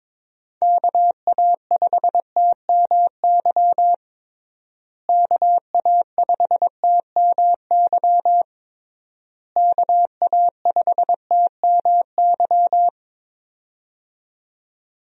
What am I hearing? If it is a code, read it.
Morse code, heard as a keyed tone, about 22 wpm, 700 Hz. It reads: KA5TMY KA5TMY KA5TMY
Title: USA Call Signs - 6 Characters - Repeated 3x - 22wpm